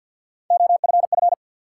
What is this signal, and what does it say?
Morse code, heard as a keyed tone, about 52 wpm, 700 Hz. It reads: OFF
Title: Sentences from Top 200 Words - New Word Drill - 50wpm